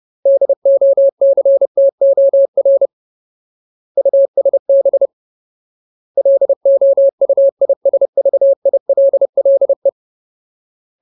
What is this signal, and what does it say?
Morse code, heard as a keyed tone, about 30 wpm, 550 Hz. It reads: DOCTOR USB LOUISVILLE
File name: Ultimate Word List / 30WPM Ultimate